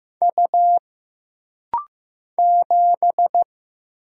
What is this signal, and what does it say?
Morse code, heard as a keyed tone, about 15 wpm, 700 Hz. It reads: U 7
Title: Single Letter-Number Alt - Rapid-Fire 15wpm